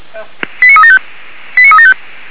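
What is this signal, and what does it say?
Five-tone selective call (ZVEI), heard as quick runs of five tones, tones of about 70 ms. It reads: 89375 89375
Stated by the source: ZVEI five-tone selective call received by SDR.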